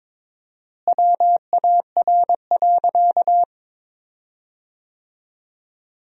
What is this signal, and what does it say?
Morse code, heard as a keyed tone, about 21 wpm, 700 Hz. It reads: WAR.
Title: 4x Word-Spacing - Sentences from Top 400 Words - 22wpm